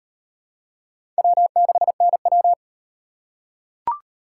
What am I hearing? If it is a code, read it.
Morse code, heard as a keyed tone, about 36 wpm, 700 Hz. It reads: W6NW